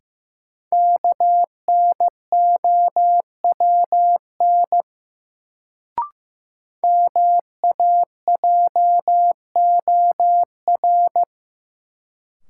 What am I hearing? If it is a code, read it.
Morse code, heard as a keyed tone, about 15 wpm, 700 Hz. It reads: KNOWN MAJOR
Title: Sentences from Top 600 Words - New Word Drill - 15wpm